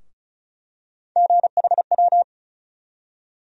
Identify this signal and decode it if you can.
Morse code, heard as a keyed tone, about 34 wpm, 700 Hz. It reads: GHW
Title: USA Call Sign Postfixes - 3 Letters 35wpm